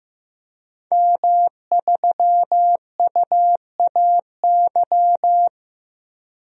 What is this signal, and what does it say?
Morse code, heard as a keyed tone, about 15 wpm, 700 Hz. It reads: M3UAY
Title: DXCC-100 Call Signs - 5 Characters - 15wpm